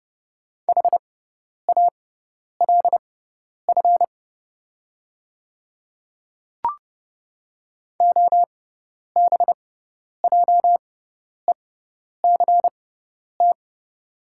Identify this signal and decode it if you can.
Morse code, heard as a keyed tone, about 30 wpm, spaced as for 10 wpm, 700 Hz. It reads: HALF OBJECT
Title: 6x Character Spacing - Top 500 Words - 30wpm